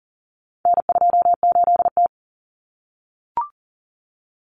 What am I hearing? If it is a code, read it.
Morse code, heard as a keyed tone, about 40 wpm, 700 Hz. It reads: N28T
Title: Common Cut Numbers - 4-Digit 40wpm